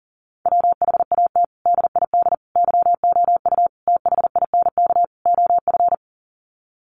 Morse code, heard as a keyed tone, about 40 wpm, 700 Hz. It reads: WHAT DID YOU THINK OF
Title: Sets of 5 Words Encore2 40wpm